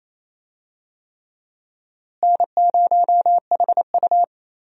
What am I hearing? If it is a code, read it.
Morse code, heard as a keyed tone, about 27 wpm, 700 Hz. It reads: N0HU
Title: USA Call Signs - 4 Characters - Repeated 3x - 28wpm